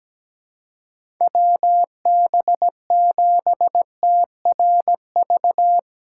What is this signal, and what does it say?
Morse code, heard as a keyed tone, about 17 wpm, 700 Hz. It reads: WB7TRV
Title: USA Call Signs - 4x Character Spacing - 17wpm